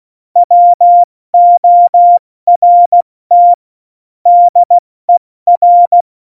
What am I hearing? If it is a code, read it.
Morse code, heard as a keyed tone, about 16 wpm, 700 Hz. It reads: WORT DER